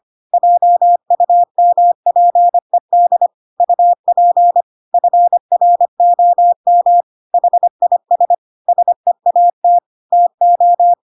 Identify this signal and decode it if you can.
Morse code, heard as a keyed tone, about 25 wpm, 700 Hz. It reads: JUMPED UP FROM HIS SEAT TO
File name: Pinocchio / Pinocchio ch7 25 25 ik1hge